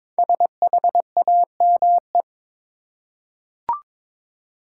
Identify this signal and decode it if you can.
Morse code, heard as a keyed tone, about 23 wpm, 700 Hz. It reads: SHAME
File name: 5 Letter Words - First 3 Letters Given - 22wpm